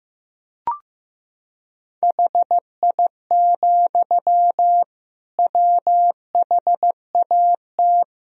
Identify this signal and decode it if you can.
Morse code, heard as a keyed tone, about 15 wpm, 700 Hz. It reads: HI, WHAT
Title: Sentences from Top 1000 Words - Review Part 3 - 15wpm